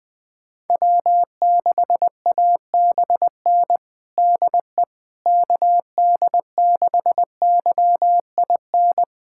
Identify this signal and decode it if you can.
Morse code, heard as a keyed tone, about 21 wpm, 700 Hz. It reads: W6ABN DE KD6YIN
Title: USA Callsign DE Callsign 20wpm